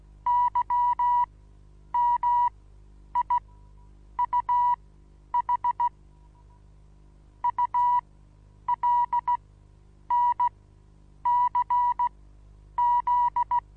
Morse code, heard as a keyed tone, about 17 wpm, spaced as for 9 wpm, 970 Hz. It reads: YMIUH ULNCZ
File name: Onlinelehrgang / morsen / DJ4UF Morsen25